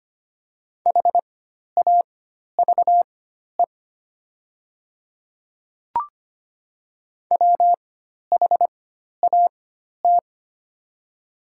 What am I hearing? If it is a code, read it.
Morse code, heard as a keyed tone, about 25 wpm, spaced as for 12 wpm, 700 Hz. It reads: HAVE WHAT
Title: 4x Character Spacing - Top 100 Words - 25wpm